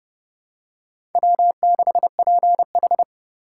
Morse code, heard as a keyed tone, about 30 wpm, 700 Hz. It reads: W6PH
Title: USA Call Signs - 2x Character Spacing - 30wpm